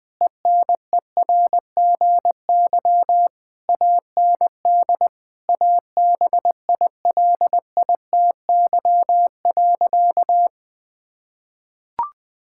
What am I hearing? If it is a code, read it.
Morse code, heard as a keyed tone, about 20 wpm, 700 Hz. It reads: ENERGY AND ABILITY.
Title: Sentences from Top 1000 Words - Review Part 6 - 20wpm